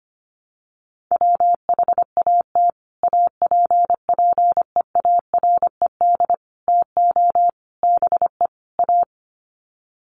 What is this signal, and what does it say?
Morse code, heard as a keyed tone, about 25 wpm, 700 Hz. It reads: WHAT APPEARED TO BE A
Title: Sets of 5 Words Encore2 25wpm